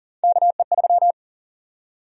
Morse code, heard as a keyed tone, about 40 wpm, 700 Hz. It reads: KE3
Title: USA Call Sign Prefixes - Repeated 2x - 40wpm